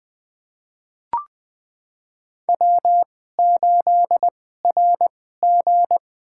Morse code, heard as a keyed tone, about 20 wpm, spaced as for 15 wpm, 700 Hz. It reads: W8RG
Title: USA Call Signs - 4 Characters - 2x Character Spacing - 20wpm